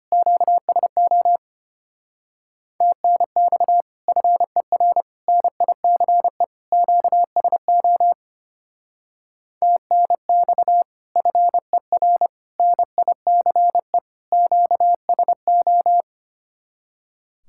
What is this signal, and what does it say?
Morse code, heard as a keyed tone, about 32 wpm, 700 Hz. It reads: QSO TNX FER NICE QSO TNX FER NICE QSO